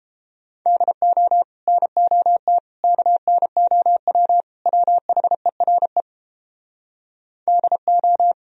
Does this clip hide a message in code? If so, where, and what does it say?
Yes: Morse code from 0.7 to 8.3 s, heard as a keyed tone, about 29 wpm, 700 Hz. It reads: DO NOT KNOW WHERE DO